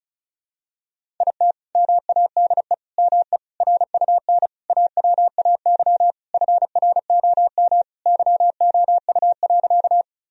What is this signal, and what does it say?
Morse code, heard as a keyed tone, about 36 wpm, 700 Hz. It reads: IT MADE ME RUN AWAY FROM YOU.
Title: Sentences from Top 1000 Words - Review Part 1 - 35wpm